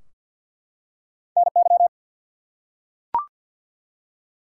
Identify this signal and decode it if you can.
Morse code, heard as a keyed tone, about 50 wpm, 700 Hz. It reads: NY
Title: US States and Canadian Provinces 50wpm